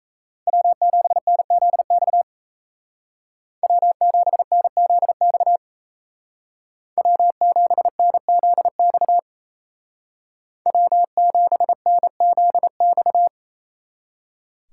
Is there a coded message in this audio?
Yes: Morse code from 0.5 to 13.3 s, heard as a keyed tone, about 36 wpm, 700 Hz. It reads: W7NZX W7NZX W7NZX W7NZX